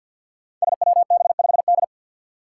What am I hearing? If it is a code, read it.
Morse code, heard as a keyed tone, about 53 wpm, 700 Hz. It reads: SWB5D
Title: Five Challenging Letter-Number Combinations 50wpm